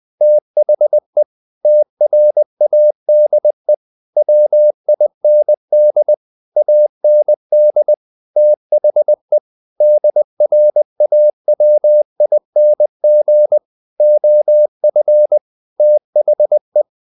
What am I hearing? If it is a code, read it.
Morse code, heard as a keyed tone, about 20 wpm, 600 Hz. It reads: THE TRADE WIND AND THE DRAWING OF THE
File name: the old man and the sea part 2 20wpm